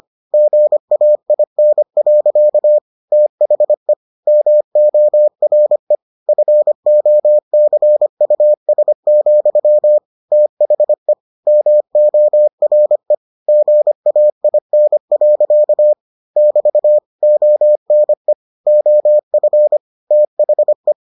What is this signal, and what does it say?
Morse code, heard as a keyed tone, about 25 wpm, 600 Hz. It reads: GAIN. THE MORE FOCUS, THE MORE GAIN. <BT> ONE OF THE